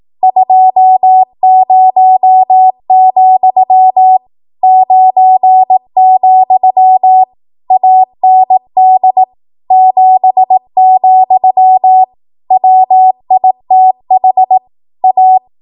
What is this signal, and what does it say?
Morse code, heard as a keyed tone, about 18 wpm, 750 Hz. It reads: 20, 9, AND 7, WITH A
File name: Archive / 18 WPM / 18WPM